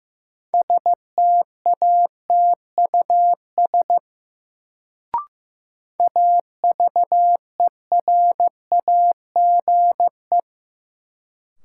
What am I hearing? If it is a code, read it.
Morse code, heard as a keyed tone, about 15 wpm, 700 Hz. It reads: STATUS AVERAGE
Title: Sentences from Top 1400 Words - New Word Drill Alt - 15wpm